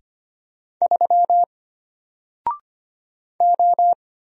Morse code, heard as a keyed tone, about 25 wpm, 700 Hz. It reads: 3 O
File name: Single-Letter-Number-25wpm